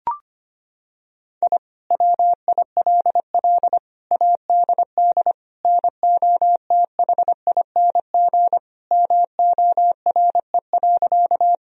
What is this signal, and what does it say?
Morse code, heard as a keyed tone, about 25 wpm, 700 Hz. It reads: I WILL ADD NOTHING MORE.
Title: Sentences from Top 1000 Words - Review Part 1 - 25wpm